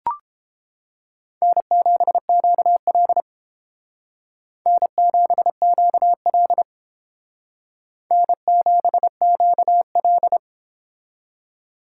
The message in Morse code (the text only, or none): N7QL N7QL N7QL